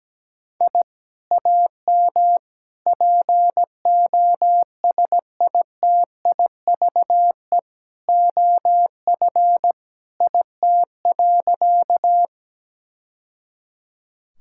Morse code, heard as a keyed tone, about 17 wpm, 700 Hz. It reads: I AM POSITIVE OF IT.
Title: Sentences from Top 1000 Words - Review Part 5 - 17wpm